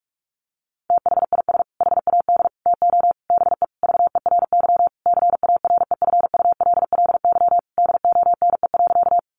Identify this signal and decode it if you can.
Morse code, heard as a keyed tone, about 44 wpm, 700 Hz. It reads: THIS HAD TO BE VERY CAREFULLY DONE.
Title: Sentences from Top 1300 Words 45wpm